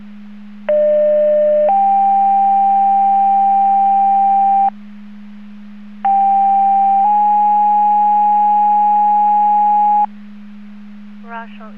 Two-tone page, heard as a one-second tone, then a three-second tone, about 600 then 790 Hz, about 790 then 830 Hz. It is heard from 0.7 to 10.0 s.